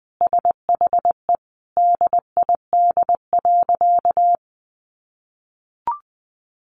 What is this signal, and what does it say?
Morse code, heard as a keyed tone, about 20 wpm, 700 Hz. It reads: SHE DID.
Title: Sentences from Top 100 Words 20wpm